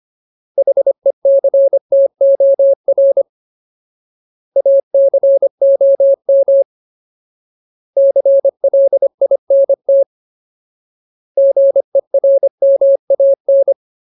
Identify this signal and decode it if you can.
Morse code, heard as a keyed tone, about 25 wpm, 550 Hz. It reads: HECTOR ACOM CLINT GERMAN
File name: Ultimate Word List / 25WPM Ultimate